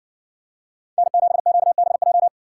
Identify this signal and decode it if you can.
Morse code, heard as a keyed tone, about 60 wpm, 700 Hz. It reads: N7QBJ